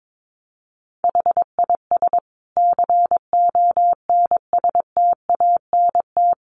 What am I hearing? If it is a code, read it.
Morse code, heard as a keyed tone, about 22 wpm, 700 Hz. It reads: HIS CONSTANT
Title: Sentences from Top 1500 Words 22wpm